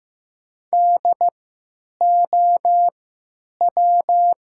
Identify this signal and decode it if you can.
Morse code, heard as a keyed tone, about 15 wpm, spaced as for 9 wpm, 700 Hz. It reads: DOW
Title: 3x Character Spacing - Top 100 Words - 15wpm